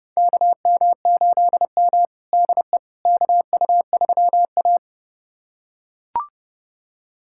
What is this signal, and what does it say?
Morse code, heard as a keyed tone, about 30 wpm, 700 Hz. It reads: KM8M DE KU3A
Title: USA Callsign DE Callsign - 4 Characters - 30wpm